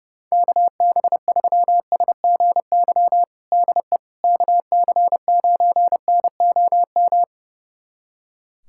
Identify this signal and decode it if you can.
Morse code, heard as a keyed tone, about 30 wpm, 700 Hz. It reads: KB3SGY DE KC9NOM
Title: USA Callsign DE Callsign - 6 Characters - 30wpm